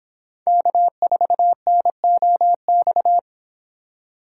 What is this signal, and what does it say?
Morse code, heard as a keyed tone, about 30 wpm, 700 Hz. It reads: K4NOX